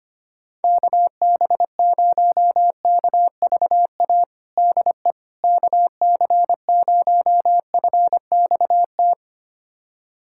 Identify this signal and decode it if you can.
Morse code, heard as a keyed tone, about 25 wpm, 700 Hz. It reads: KB0KVA DE KC0FXT
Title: USA Callsign DE Callsign - 6 Characters - 25wpm